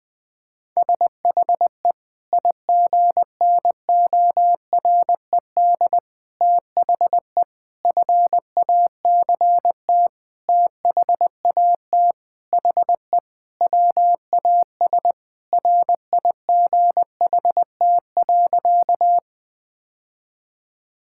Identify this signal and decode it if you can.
Morse code, heard as a keyed tone, about 21 wpm, 700 Hz. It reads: SHE IGNORED THE FACT THAT HE WAS RIGHT.
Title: Sentences with 5-Word Phrases - 20wpm